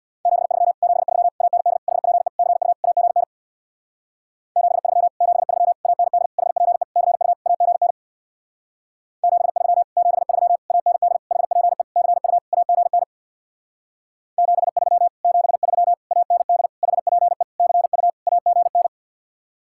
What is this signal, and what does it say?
Morse code, heard as a keyed tone, about 65 wpm, 700 Hz. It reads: 73 73 AND HPE CU AGN 73 73 AND HPE CU AGN 73 73 AND HPE CU AGN 73 73 AND HPE CU AGN